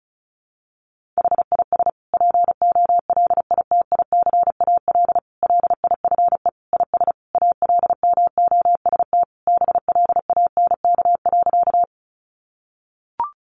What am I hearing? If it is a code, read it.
Morse code, heard as a keyed tone, about 34 wpm, 700 Hz. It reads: HIS POLITICAL LIFE IS ALMOST BLANK.
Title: Sentences from Top 1500 Words 35wpm